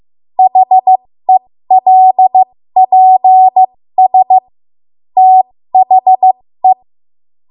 Morse code, heard as a keyed tone, about 15 wpm, spaced as for 13 wpm, 750 Hz. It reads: HELPS THE
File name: Morse / 13WPM